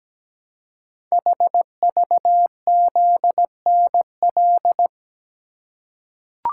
Morse code, heard as a keyed tone, about 17 wpm, 700 Hz. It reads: HVZNL
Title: Code Groups - 5 Letters - 17wpm